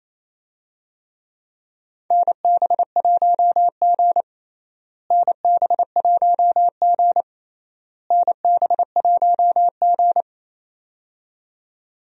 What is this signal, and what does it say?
Morse code, heard as a keyed tone, about 28 wpm, 700 Hz. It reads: NB1G NB1G NB1G
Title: USA Call Signs - 4 Characters - Repeated 3x - 28wpm